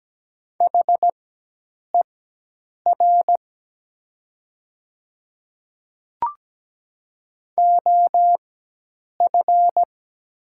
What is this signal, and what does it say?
Morse code, heard as a keyed tone, about 17 wpm, spaced as for 8 wpm, 700 Hz. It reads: HER OF
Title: 4x Character Spacing - Top 300 Words - 17wpm